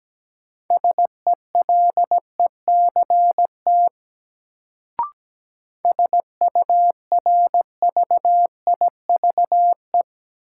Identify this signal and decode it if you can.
Morse code, heard as a keyed tone, about 17 wpm, 700 Hz. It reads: SELECT SURVIVE
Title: Sentences from Top 1300 Words - New Word Drill - 17wpm